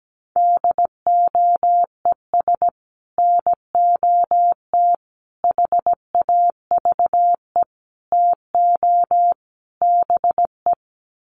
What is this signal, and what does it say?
Morse code, heard as a keyed tone, about 17 wpm, 700 Hz. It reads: DOES NOT HAVE TO BE
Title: Sets of 5 Words 17wpm